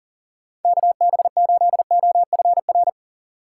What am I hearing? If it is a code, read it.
Morse code, heard as a keyed tone, about 40 wpm, 700 Hz. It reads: KB8OFR
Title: USA Call Signs - 6 Characters - 6x Character Spacing - 40wpm